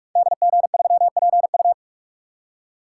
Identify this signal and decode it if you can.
Morse code, heard as a keyed tone, about 44 wpm, 700 Hz. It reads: DG3PU
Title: Code Groups - 5 Letter:Numbers - 45wpm